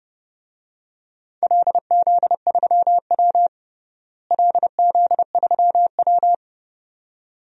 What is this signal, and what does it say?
Morse code, heard as a keyed tone, about 32 wpm, 700 Hz. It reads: LZ3W LZ3W
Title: DXCC-100 Call Signs - 4 Characters - Repeated 2x - 30wpm